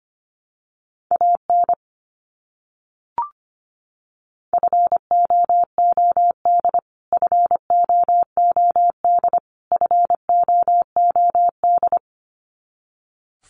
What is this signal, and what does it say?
Morse code, heard as a keyed tone, about 25 wpm, 700 Hz. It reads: AN FOOD FOOD FOOD
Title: Top 300 Words Repeated 25wpm